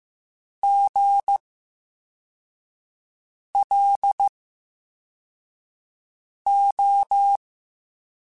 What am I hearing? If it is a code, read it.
Morse code, heard as a keyed tone, about 15 wpm, spaced as for 4 wpm, 800 Hz. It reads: GLO
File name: Special INGs / Special INGs-mp3s / Slow ing @13